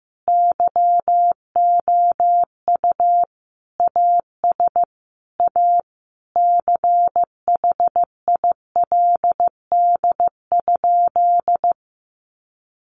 Morse code, heard as a keyed tone, about 15 wpm, 700 Hz. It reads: YOU AS A CHILD?
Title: Sentences from Top 1300 Words 15wpm